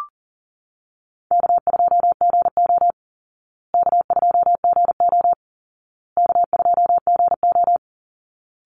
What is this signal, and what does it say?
Morse code, heard as a keyed tone, about 40 wpm, 700 Hz. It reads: K2GO K2GO K2GO
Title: Call Signs Repeated 2x - 40wpm